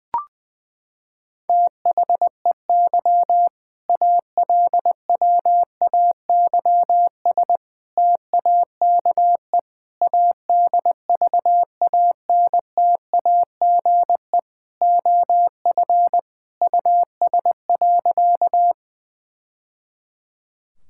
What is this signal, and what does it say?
Morse code, heard as a keyed tone, about 20 wpm, 700 Hz. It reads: THEY ALWAYS TAKE ADVANTAGE OF US.
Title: Sentences from Top 1000 Words - Review Part 6 - 20wpm